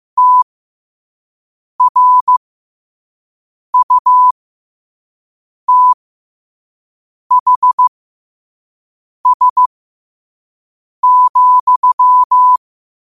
Morse code, heard as a keyed tone, about 15 wpm, spaced as for 5 wpm, 1000 Hz. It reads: TRUTHS,
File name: qotdfiles / 5wpm